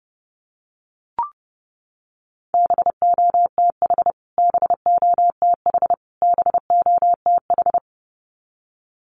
Morse code, heard as a keyed tone, about 30 wpm, 700 Hz. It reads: BOTH BOTH BOTH
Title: Top 500 Words Repeated 30wpm